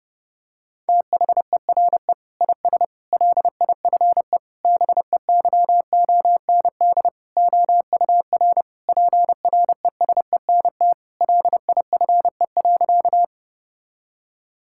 Morse code, heard as a keyed tone, about 32 wpm, 700 Hz. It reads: THERE IS LIFE BEYOND OUR PRESENT LIFE.
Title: Sentences from Top 1000 Words - Review Part 6 - 30wpm